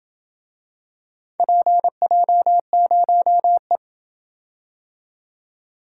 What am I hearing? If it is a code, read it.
Morse code, heard as a keyed tone, about 28 wpm, 700 Hz. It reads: PJ0E